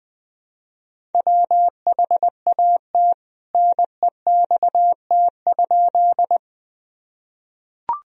Morse code, heard as a keyed tone, about 19 wpm, 700 Hz. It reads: WHAT NEXT?
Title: Sentences from Top 1000 Words - Review Part 1 - 20wpm